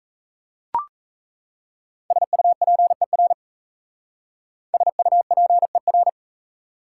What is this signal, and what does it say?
Morse code, heard as a keyed tone, about 35 wpm, 700 Hz. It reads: SUPER SUPER